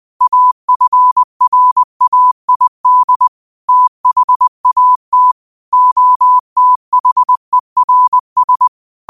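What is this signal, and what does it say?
Morse code, heard as a keyed tone, about 20 wpm, 1000 Hz. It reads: AFRAID THAT OTHERS
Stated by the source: Quote of the day sent in Morse at 20 words per minute.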